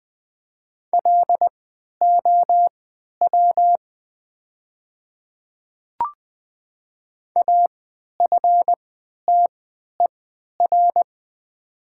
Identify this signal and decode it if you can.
Morse code, heard as a keyed tone, about 20 wpm, spaced as for 11 wpm, 700 Hz. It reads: LOW AFTER
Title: 3x Character Spacing - Top 200 Words - 20wpm